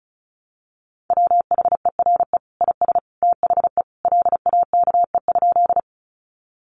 Morse code, heard as a keyed tone, about 34 wpm, 700 Hz. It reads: WHERE IS THE LAKE?
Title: Sentences from Top 1100 Words 35wpm